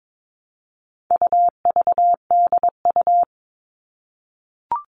Morse code, heard as a keyed tone, about 22 wpm, 700 Hz. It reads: UVDU
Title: All Cut Numbers - 4-Digit 22wpm